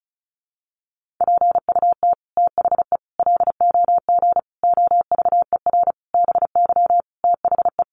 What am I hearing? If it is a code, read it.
Morse code, heard as a keyed tone, about 34 wpm, 700 Hz. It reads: PUT THE LOG OVER BY THE
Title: Sentences from Top 1400 Words 35wpm